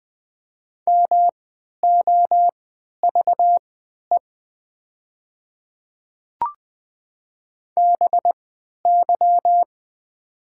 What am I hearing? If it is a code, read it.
Morse code, heard as a keyed tone, about 20 wpm, spaced as for 11 wpm, 700 Hz. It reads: MOVE BY
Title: 3x Character Spacing - Top 300 Words - 20wpm